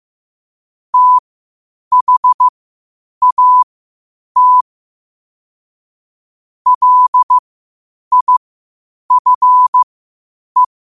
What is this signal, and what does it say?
Morse code, heard as a keyed tone, about 15 wpm, spaced as for 8 wpm, 1000 Hz. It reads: THAT LIFE
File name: qotdfiles / 8wpm